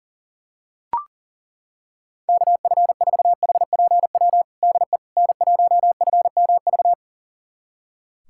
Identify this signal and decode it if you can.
Morse code, heard as a keyed tone, about 40 wpm, 700 Hz. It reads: KF4HPW DE N1FMV
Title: USA Callsign DE Callsign 40wpm